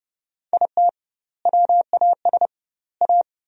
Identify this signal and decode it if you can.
Morse code, heard as a keyed tone, about 32 wpm, 700 Hz. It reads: IT WAS A
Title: 2x Word-Spacing - Sentences from Top 500 Words - 30wpm